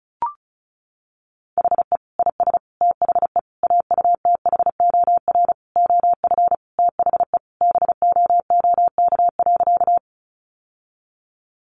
Morse code, heard as a keyed tone, about 34 wpm, 700 Hz. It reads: HE IS THE AUTHOR OF THE BOOK.
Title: Sentences from Top 1200 Words 35wpm